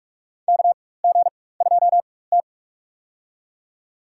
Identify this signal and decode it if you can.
Morse code, heard as a keyed tone, about 43 wpm, spaced as for 21 wpm, 700 Hz. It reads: KG2T
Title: USA Call Signs - 4 Characters - 4x Character Spacing - 45wpm